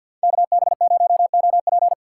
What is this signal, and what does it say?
Morse code, heard as a keyed tone, about 62 wpm, 700 Hz. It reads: KB0OP
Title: Speed Racing - USA Call Signs 50wpm